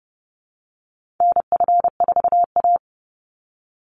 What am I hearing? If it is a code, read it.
Morse code, heard as a keyed tone, about 30 wpm, 700 Hz. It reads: NF4A
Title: Call Signs - 30wpm